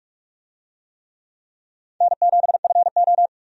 Morse code, heard as a keyed tone, about 46 wpm, 700 Hz. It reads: N7FO
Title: USA Call Signs - 4 Characters - Repeated 3x - 45wpm